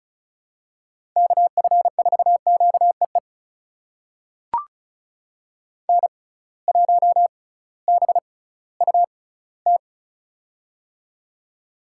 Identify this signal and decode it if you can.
Morse code, heard as a keyed tone, about 34 wpm, spaced as for 19 wpm, 700 Hz. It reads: KF4QEE N1BUT
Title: USA Call Signs - 6x Character Spacing - 35wpm